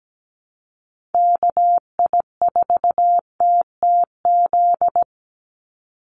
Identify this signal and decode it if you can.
Morse code, heard as a keyed tone, about 17 wpm, 700 Hz. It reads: KI4TTZ
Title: USA Call Signs 17wpm